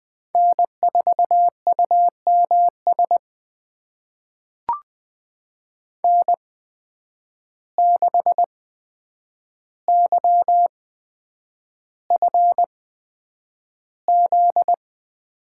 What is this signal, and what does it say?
Morse code, heard as a keyed tone, about 20 wpm, spaced as for 9 wpm, 700 Hz. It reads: N4UMS N6YFZ
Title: USA Call Signs - 8x Character Spacing - 20wpm